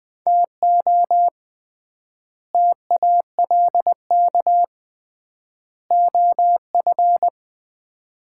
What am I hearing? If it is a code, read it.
Morse code, heard as a keyed tone, about 20 wpm, 700 Hz. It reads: TO TALK OF
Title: 3x Word-Spacing - Sets of 5 Words - 20wpm